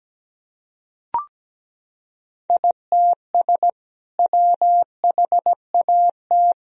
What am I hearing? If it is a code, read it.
Morse code, heard as a keyed tone, about 17 wpm, 700 Hz. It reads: ITS WHAT
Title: Sentences from Top 1000 Words - Review Part 3 - 17wpm